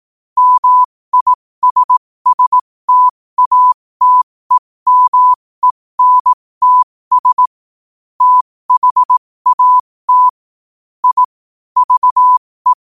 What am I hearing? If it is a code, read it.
Morse code, heard as a keyed tone, about 18 wpm, spaced as for 15 wpm, 1000 Hz. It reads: MISSTATEMENTS THAT I VE
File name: qotdfiles / 15wpm